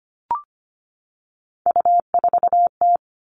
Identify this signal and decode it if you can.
Morse code, heard as a keyed tone, about 25 wpm, 700 Hz. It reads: U4T
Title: All Cut Numbers - 3-Digit 25wpm